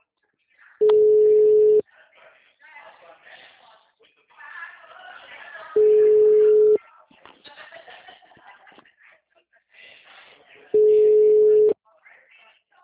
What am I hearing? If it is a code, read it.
Ringback tone (Europe, 425 Hz), with speech.